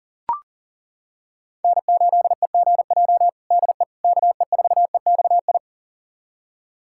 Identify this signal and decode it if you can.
Morse code, heard as a keyed tone, about 40 wpm, 700 Hz. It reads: N8EGJ DE KE4EXI
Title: USA Callsign DE Callsign 40wpm